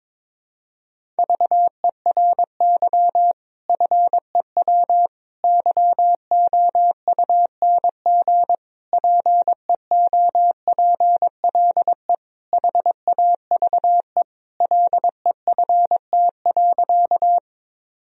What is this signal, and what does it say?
Morse code, heard as a keyed tone, about 21 wpm, 700 Hz. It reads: VERY FEW YOUNG PEOPLE HAVE LEFT.
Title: Sentences from Top 1000 Words - Review Part 1 - 22wpm